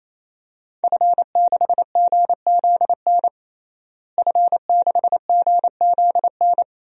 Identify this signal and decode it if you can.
Morse code, heard as a keyed tone, about 28 wpm, 700 Hz. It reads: F6GZN F6GZN
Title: DXCC-100 Call Signs - 5 Characters - Repeated 3x - 28wpm